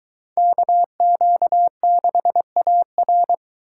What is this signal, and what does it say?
Morse code, heard as a keyed tone, about 22 wpm, 700 Hz. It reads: KQ6AR